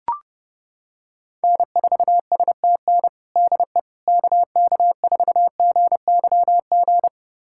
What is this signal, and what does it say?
Morse code, heard as a keyed tone, about 28 wpm, 700 Hz. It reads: N4STN DE KK4GYG
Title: USA Callsign DE Callsign 30wpm